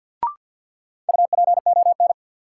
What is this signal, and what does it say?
Morse code, heard as a keyed tone, about 52 wpm, 700 Hz. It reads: UPON
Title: Sentences from Top 1100 Words - New Word Drill Alt - 50wpm